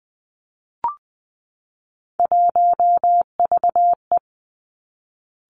Morse code, heard as a keyed tone, about 20 wpm, 700 Hz. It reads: 1VE
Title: Three Challenging Letter-Number Combinations 20wpm